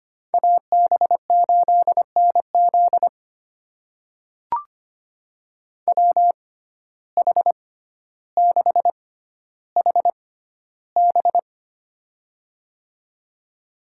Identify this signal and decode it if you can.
Morse code, heard as a keyed tone, about 25 wpm, spaced as for 13 wpm, 700 Hz. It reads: AB8NZ WH6HB
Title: USA Call Signs - 5 Characters - 6x Character Spacing - 25wpm